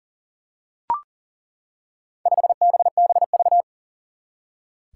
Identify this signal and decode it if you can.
Morse code, heard as a keyed tone, about 40 wpm, 700 Hz. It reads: 5BBV
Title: Four Challenging Letter-Number Combinations 40wpm